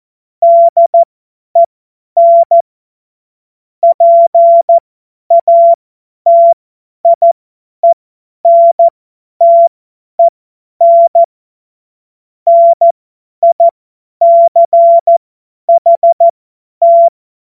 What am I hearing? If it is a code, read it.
Morse code, heard as a keyed tone, about 14 wpm, spaced as for 10 wpm, 680 Hz. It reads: DEN PATIENTEN NICHT
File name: Morsekurs / Witze / Witz 11. Juli 2023 50 F70